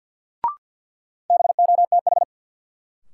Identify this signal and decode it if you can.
Morse code, heard as a keyed tone, about 52 wpm, 700 Hz. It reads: BOTH
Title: Sentences from Top 300 Words - New Word Drill - 50wpm